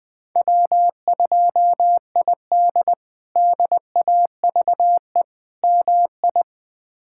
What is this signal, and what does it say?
Morse code, heard as a keyed tone, about 20 wpm, 700 Hz. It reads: W2ID DAVE MI
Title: K1USN SST QSO Practice - 20wpm